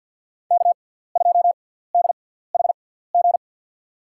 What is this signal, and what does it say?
Morse code, heard as a keyed tone, about 50 wpm, spaced as for 17 wpm, 700 Hz. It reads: K2DHG
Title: USA Call Signs - 5 Characters - 6x Character Spacing - 50wpm